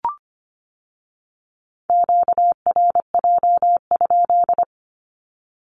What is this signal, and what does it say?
Morse code, heard as a keyed tone, about 25 wpm, 700 Hz. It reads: QRJ?
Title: Q-Codes with Spoken Meaning - 25wpm